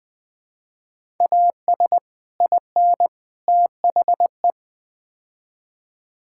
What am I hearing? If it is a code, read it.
Morse code, heard as a keyed tone, about 20 wpm, 700 Hz. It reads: AS IN THE
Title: Build A Phrase - 5-Word Phrases - 20wpm